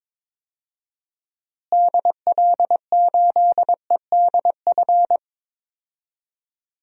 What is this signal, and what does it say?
Morse code, heard as a keyed tone, about 22 wpm, 700 Hz. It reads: DL8EDF
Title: DXCC-100 Call Signs - 6 Characters - Repeated 3x - 22wpm